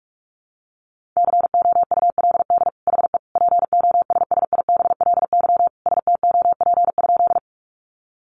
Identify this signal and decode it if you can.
Morse code, heard as a keyed tone, about 46 wpm, 700 Hz. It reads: COULD HE POSSIBLY STOP?
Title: Sentences from Top 1300 Words 45wpm